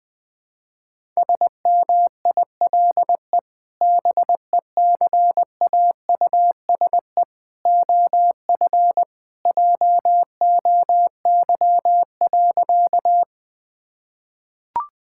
Morse code, heard as a keyed tone, about 20 wpm, 700 Hz. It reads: SMILE BECAUSE OF JOY.
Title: Sentences with 2-Word Phrases - Spoken Beginning - 20wpm